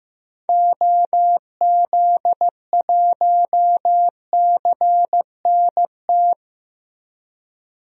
Morse code, heard as a keyed tone, about 15 wpm, 700 Hz. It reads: OZ1CNT
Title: DXCC-100 Call Signs - 6 Characters - Repeated 2x - 15wpm